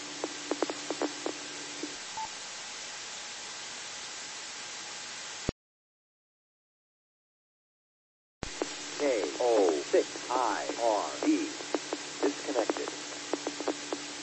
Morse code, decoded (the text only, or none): none